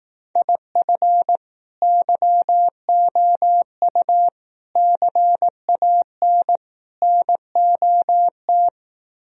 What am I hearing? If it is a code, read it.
Morse code, heard as a keyed tone, about 18 wpm, 700 Hz. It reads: IF YOU CAN NOT
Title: Speed-Racing - Sets of 4 Words 15wpm